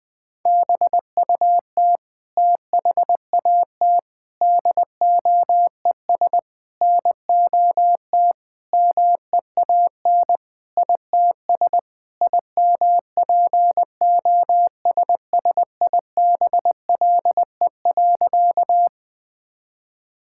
Morse code, heard as a keyed tone, about 20 wpm, 700 Hz. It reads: BUT THAT DOES NOT MEAN ITS IMPOSSIBLE.
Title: Sentences with 5-Word Phrases - 20wpm